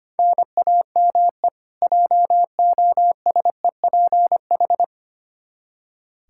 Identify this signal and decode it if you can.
Morse code, heard as a keyed tone, about 25 wpm, 700 Hz. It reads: NAME JOSEPH
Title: Practice QSO - Exchange 1 - Beginning - 25wpm